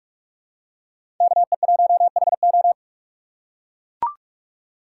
Morse code, heard as a keyed tone, about 44 wpm, 690 Hz. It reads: KE1HO
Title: USA Call Signs - 5 Characters - 2x Character Spacing - 45wpm